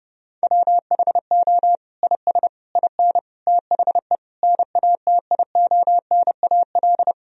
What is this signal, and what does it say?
Morse code, heard as a keyed tone, about 29 wpm, 700 Hz. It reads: WHO IS IN THE NATIONAL
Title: Sentences from Top 1000 Words - Review Part 4 - 30wpm